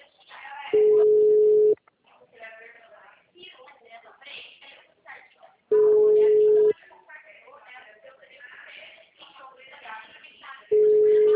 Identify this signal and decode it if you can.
Ringback tone (Europe, 425 Hz), with speech.
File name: out- -2017- - - .480